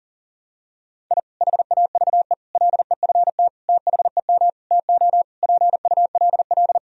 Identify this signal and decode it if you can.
Morse code, heard as a keyed tone, about 40 wpm, 700 Hz. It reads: I HAVE LEFT THEM TO PULL
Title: Sentences from Top 1000 Words - Review Part 2 - 40wpm